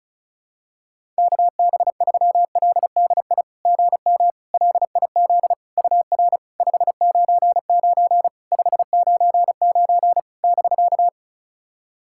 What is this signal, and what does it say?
Morse code, heard as a keyed tone, about 34 wpm, 690 Hz. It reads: KB3LDI GM LIZ UR 599 599 <BK>
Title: SOTA QSO Practice 35wpm